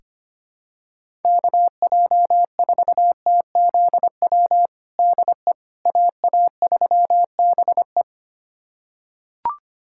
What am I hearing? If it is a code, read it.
Morse code, heard as a keyed tone, about 25 wpm, 700 Hz. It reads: KJ4TZW DE AA3BE
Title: USA Callsign DE Callsign 25wpm